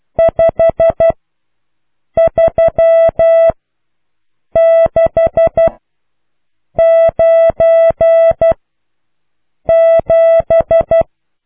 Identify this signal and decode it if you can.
Morse code, heard as a keyed tone, about 12 wpm, spaced as for 6 wpm, 660 Hz. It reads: 53697